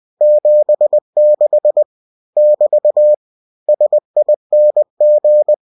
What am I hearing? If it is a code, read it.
Morse code, heard as a keyed tone, about 20 wpm, 600 Hz. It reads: 76 <BT> SING
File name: cw-books / winnie-the-pooh / winnie-20wpm-page-0076